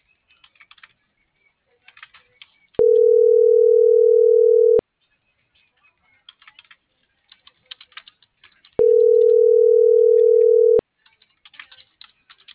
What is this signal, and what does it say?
Ringback tone (North America), 440 + 480 Hz.